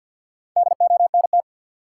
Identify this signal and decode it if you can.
Morse code, heard as a keyed tone, about 52 wpm, 700 Hz. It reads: DONT
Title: Sentences from Top 300 Words - New Word Drill - 50wpm